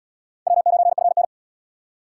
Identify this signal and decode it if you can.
Morse code, heard as a keyed tone, about 72 wpm, 700 Hz. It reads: W9FN